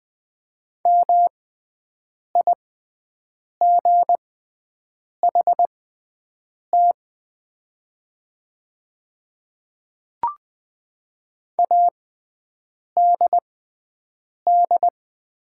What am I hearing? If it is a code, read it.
Morse code, heard as a keyed tone, about 20 wpm, spaced as for 7 wpm, 700 Hz. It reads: MIGHT ADD
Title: 6x Character Spacing - Top 300 Words - 20wpm